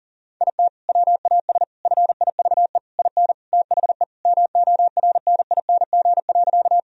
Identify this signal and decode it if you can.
Morse code, heard as a keyed tone, about 40 wpm, 700 Hz. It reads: IT WAS FIVE IN THE MORNING.
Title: Sentences from Top 1000 Words - Review Part 2 - 40wpm